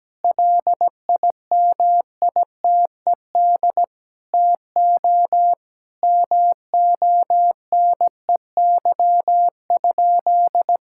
Morse code, heard as a keyed tone, about 17 wpm, 700 Hz. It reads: LIMITED TO MONEY?
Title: Sentences from Top 1000 Words - Review Part 6 - 17wpm